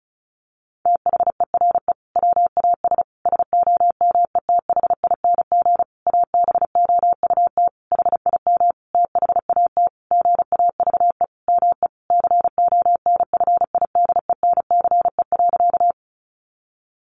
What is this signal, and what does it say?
Morse code, heard as a keyed tone, about 34 wpm, 700 Hz. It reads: THERE WAS SOMETHING ABOUT HIM THAT GAVE ME CONFIDENCE.
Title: Sentences from Top 1500 Words 35wpm